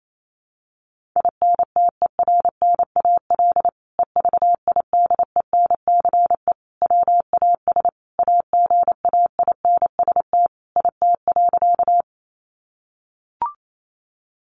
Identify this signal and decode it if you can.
Morse code, heard as a keyed tone, about 28 wpm, 700 Hz. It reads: INTERNAL EVIDENCE WAS AGAINST IT.
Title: Sentences from Top 1400 Words 28wpm